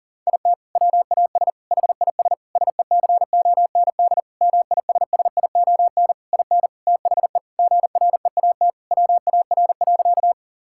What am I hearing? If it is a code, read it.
Morse code, heard as a keyed tone, about 37 wpm, 700 Hz. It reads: IT WAS HIS SECOND MISSION IN THE GREAT WAR.
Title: Sentences from Top 1600 Words 40wpm